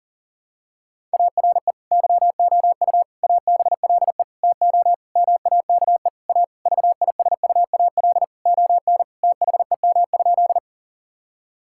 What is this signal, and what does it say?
Morse code, heard as a keyed tone, about 40 wpm, 700 Hz. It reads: ARE YOU ABLE TO MAKE A VISUAL ON THEM?
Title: Sentences from Top 1000 Words - Review Part 6 - 40wpm